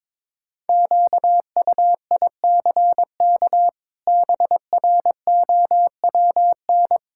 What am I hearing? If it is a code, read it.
Morse code, heard as a keyed tone, about 22 wpm, 700 Hz. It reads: QUICK BROWN
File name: practice / 22wpm / Sets of 8 Words - 22wpm